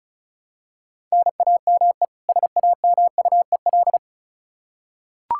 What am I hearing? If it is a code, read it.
Morse code, heard as a keyed tone, about 34 wpm, 700 Hz. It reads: NAME SAMUEL
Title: Practice QSO - Exchange 1 - Beginning - 35wpm